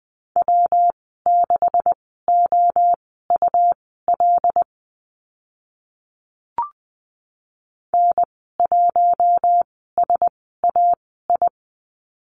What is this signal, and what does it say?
Morse code, heard as a keyed tone, about 20 wpm, spaced as for 14 wpm, 700 Hz. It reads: W6OUL N1SAI
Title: 179 Lesson 29g - Slash - Call Signs 20wpm